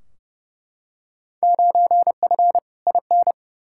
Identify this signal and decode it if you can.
Morse code, heard as a keyed tone, about 31 wpm, 700 Hz. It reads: 9F IN
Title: Field Day Exchanges Encore - 30wpm